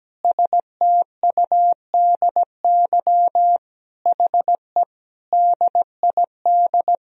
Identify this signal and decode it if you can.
Morse code, heard as a keyed tone, about 17 wpm, 700 Hz. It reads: STUDY HE DID
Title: Sentences from Top 1000 Words - Review Part 1 - 17wpm